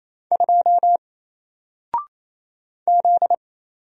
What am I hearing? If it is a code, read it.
Morse code, heard as a keyed tone, about 28 wpm, 700 Hz. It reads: 2 Z
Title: Single Letter-Number Alt 28wpm